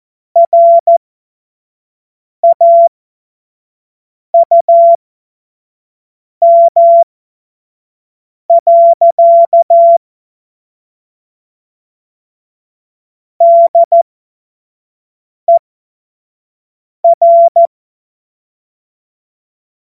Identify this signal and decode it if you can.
Morse code, heard as a keyed tone, about 14 wpm, spaced as for 5 wpm, 680 Hz. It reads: RAUM. DER
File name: Morsekurs / Witze / Witz 17. Mai 2022 25 F70